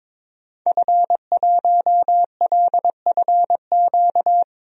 Morse code, heard as a keyed tone, about 21 wpm, 700 Hz. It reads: F1LFQ
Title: DXCC-100 Call Signs - 5 Characters - Repeated 3x - 22wpm